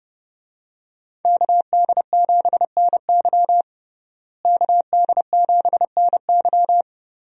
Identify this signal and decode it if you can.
Morse code, heard as a keyed tone, about 29 wpm, 700 Hz. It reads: KD7NY KD7NY
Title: USA Call Signs - 5 Characters - Repeated 3x - 30wpm